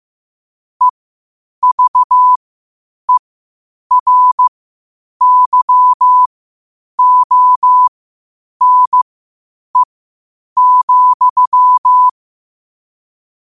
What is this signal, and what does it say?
Morse code, heard as a keyed tone, about 15 wpm, spaced as for 8 wpm, 1000 Hz. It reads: EVERYONE,
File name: qotdfiles / 8wpm